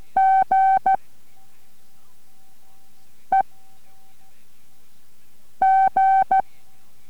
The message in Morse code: G E G